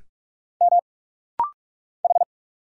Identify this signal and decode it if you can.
Morse code, heard as a keyed tone, about 45 wpm, 700 Hz. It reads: M H
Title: Single Letter-Number Alt - Rapid-Fire 45wpm